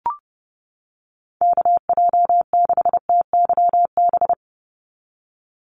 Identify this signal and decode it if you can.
Morse code, heard as a keyed tone, about 30 wpm, 700 Hz. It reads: KJ6TYB
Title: USA Call Signs 30wpm